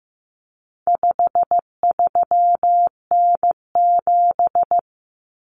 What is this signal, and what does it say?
Morse code, heard as a keyed tone, about 15 wpm, 700 Hz. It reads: 53N7
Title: Common Cut Numbers - 4-Digit 15wpm